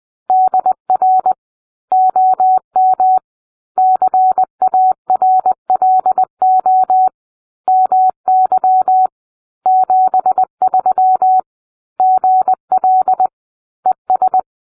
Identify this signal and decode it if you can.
Morse code, heard as a keyed tone, about 20 wpm, 750 Hz. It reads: DR OM CARLO MY 73 GL ES